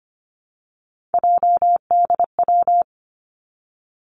Morse code, heard as a keyed tone, about 25 wpm, 700 Hz. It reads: JDW
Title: Three Challenging Letter-Number Combinations 25wpm